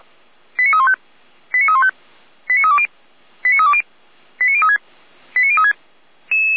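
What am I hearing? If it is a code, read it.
Five-tone selective call (ZVEI), heard as quick runs of five tones, tones of about 70 ms. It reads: 89315 89315 89320 89320 89036 89036
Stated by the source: ZVEI five-tone call received by SDR, from Feuerwehr Orlamünde audio server.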